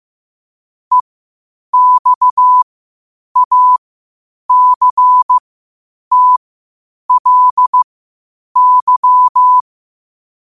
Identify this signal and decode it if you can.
Morse code, heard as a keyed tone, about 15 wpm, spaced as for 8 wpm, 1000 Hz. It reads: EXACTLY
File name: 8wpm